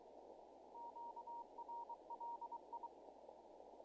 Morse code, heard as a keyed tone, about 23 wpm, 940 Hz, with noise. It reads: QRL I
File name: contests / naqp-cw-k5tr-op-k5pi- -1700